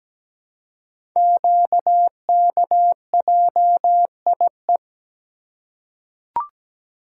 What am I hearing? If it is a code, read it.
Morse code, heard as a keyed tone, about 17 wpm, 700 Hz. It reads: QKJIE
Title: Code Groups - 5 Letters - 17wpm